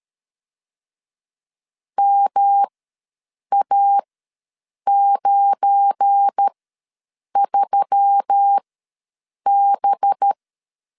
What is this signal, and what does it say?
Morse code, heard as a keyed tone, about 13 wpm, spaced as for 7 wpm, 800 Hz. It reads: MA93B
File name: LEZIONI / 7 GF9Z / 7 GF9 ws 12wpm 3344